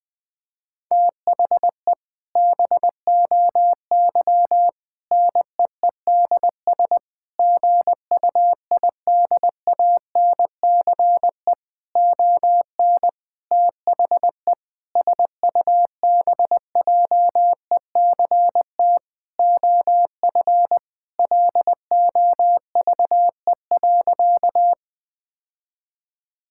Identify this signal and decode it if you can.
Morse code, heard as a keyed tone, about 20 wpm, 700 Hz. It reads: THE BOY NEEDS GUIDANCE ON THE SUBJECT OF LOVE.
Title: Sentences from Top 1600 Words 20wpm